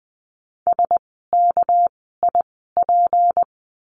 Morse code, heard as a keyed tone, about 20 wpm, spaced as for 14 wpm, 700 Hz. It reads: SKIP
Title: 199 Lesson 32f - Letter k - Words 20wpm